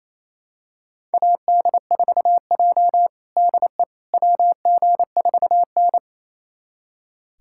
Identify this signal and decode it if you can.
Morse code, heard as a keyed tone, about 28 wpm, 700 Hz. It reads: AD4J DE WG4N
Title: USA Callsign DE Callsign - 4 Characters - 28wpm